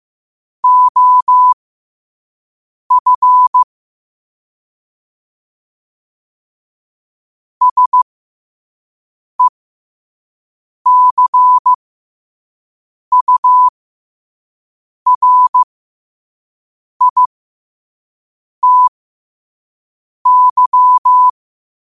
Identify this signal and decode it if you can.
Morse code, heard as a keyed tone, about 15 wpm, spaced as for 5 wpm, 1000 Hz. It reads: OF SECURITY
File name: 5wpm